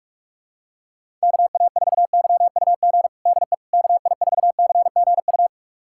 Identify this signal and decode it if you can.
Morse code, heard as a keyed tone, about 42 wpm, 700 Hz. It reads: KA4YUG DE KI4CGU
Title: USA Callsign DE Callsign - 6 Characters - 45wpm